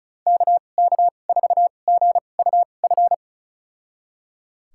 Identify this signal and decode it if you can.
Morse code, heard as a keyed tone, about 36 wpm, spaced as for 25 wpm, 700 Hz. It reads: KK4GUF
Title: USA Call Signs - 6 Characters - 2x Character Spacing - 35wpm